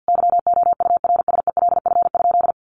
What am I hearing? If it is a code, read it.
Morse code, heard as a keyed tone, about 50 wpm, 700 Hz. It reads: YOURSELF?
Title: Sentences from Top 1500 Words 50wpm